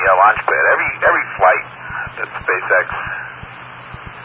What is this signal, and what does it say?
Emergency Alert System (SAME) tones, heard as harsh buzzing bursts, with speech. It is heard from 0.4 to 3.4 s.